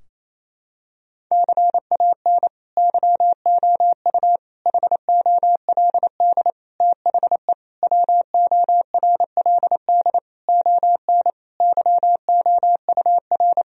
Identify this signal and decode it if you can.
Morse code, heard as a keyed tone, about 27 wpm, 700 Hz. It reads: CAN YOU HOLD THE WORLD ON YOUR
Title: Sentences from Top 1000 Words - Review Part 2 - 28wpm